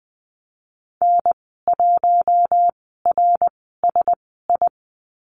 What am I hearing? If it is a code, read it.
Morse code, heard as a keyed tone, about 20 wpm, spaced as for 15 wpm, 700 Hz. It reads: N1RSI
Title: 055 Lesson 10g - Letter r - Call Signs 20wpm